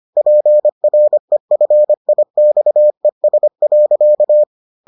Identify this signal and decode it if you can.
Morse code, heard as a keyed tone, about 25 wpm, 600 Hz. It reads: PREFIXES.